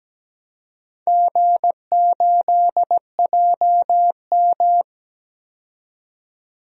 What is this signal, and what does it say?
Morse code, heard as a keyed tone, about 17 wpm, 700 Hz. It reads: G8JM
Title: DXCC-100 Call Signs - 4 Characters - 17wpm